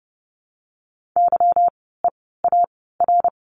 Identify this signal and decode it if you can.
Morse code, heard as a keyed tone, about 30 wpm, spaced as for 17 wpm, 700 Hz. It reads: YEAR
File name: course / 30:10wpm / 230 Lesson 37f - Letter x - Words 30wpm